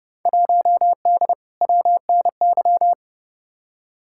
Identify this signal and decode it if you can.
Morse code, heard as a keyed tone, about 30 wpm, 700 Hz. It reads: 1D WNY
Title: Field Day Exchanges - 30wpm